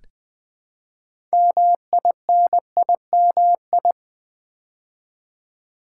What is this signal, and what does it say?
Morse code, heard as a keyed tone, about 20 wpm, 700 Hz. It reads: MINIMI
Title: Build A Word - 3 Letters At A Time - 8-Letter Words - 20wpm